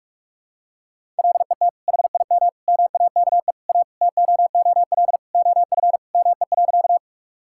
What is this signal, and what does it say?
Morse code, heard as a keyed tone, about 45 wpm, 700 Hz. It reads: LET HIM MAKE A TOOL OF ME.
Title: Sentences from Top 1000 Words - Review Part 6 - 45wpm